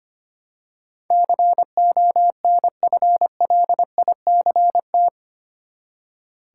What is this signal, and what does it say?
Morse code, heard as a keyed tone, about 25 wpm, 700 Hz. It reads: CONFLICT
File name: Eight Letter Words - 25wpm